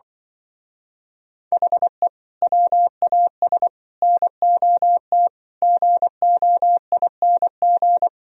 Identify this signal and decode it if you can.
Morse code, heard as a keyed tone, about 24 wpm, 700 Hz. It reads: HE WAS NOT GOING
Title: Speed-Racing - Sets of 5 Words 20wpm